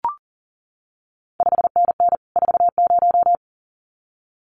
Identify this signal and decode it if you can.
Morse code, heard as a keyed tone, about 40 wpm, 700 Hz. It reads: 5NN 40
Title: CQ WW DX Contest Exchange Encore - 40wpm